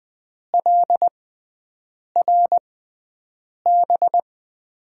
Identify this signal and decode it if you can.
Morse code, heard as a keyed tone, about 20 wpm, spaced as for 7 wpm, 700 Hz. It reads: LRB
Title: USA Call Signs - 5 Characters - 6x Character Spacing - 20wpm